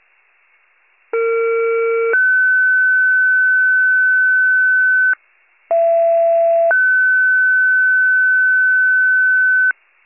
Two-tone page, heard as a one-second tone, then a three-second tone, about 460 then 1500 Hz, about 670 then 1500 Hz. It is heard from 1.1 to 9.7 s.